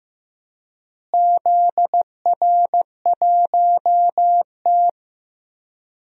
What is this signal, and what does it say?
Morse code, heard as a keyed tone, about 16 wpm, 700 Hz. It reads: ZR1T
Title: DXCC-100 Call Signs - 4 Characters - Repeated 3x - 15wpm